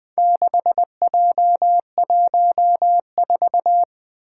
Morse code, heard as a keyed tone, about 21 wpm, 700 Hz. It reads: 6J14
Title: Four Challenging Letter-Number Combinations 20wpm